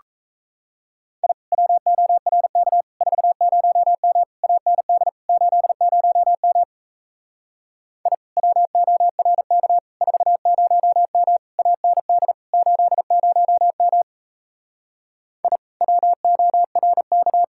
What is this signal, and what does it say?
Morse code, heard as a keyed tone, about 36 wpm, 700 Hz. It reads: I WORK 40M AND 80M I WORK 40M AND 80M I WORK